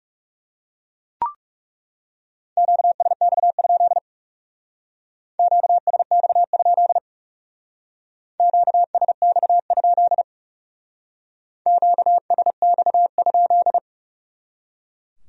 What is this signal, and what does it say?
Morse code, heard as a keyed tone, about 39 wpm, 700 Hz. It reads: QSX? QSX? QSX? QSX?